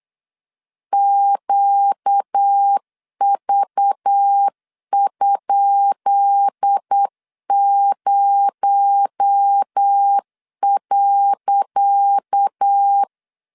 Morse code, heard as a keyed tone, about 8 wpm, 800 Hz. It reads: QV?0.
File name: LEZIONI / 10 SIGNS / 10C DOT 8wpm 3649